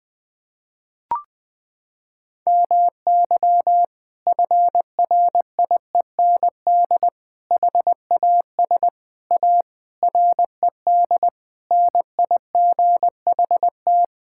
Morse code, heard as a keyed tone, about 21 wpm, 700 Hz. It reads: MY FRIEND HAS A RED NIGHT
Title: Sentences from Top 1600 Words 20wpm